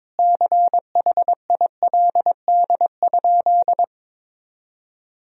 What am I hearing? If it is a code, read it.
Morse code, heard as a keyed tone, about 22 wpm, 700 Hz. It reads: CHILD?
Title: Sentences from Top 1000 Words - Review Part 1 - 22wpm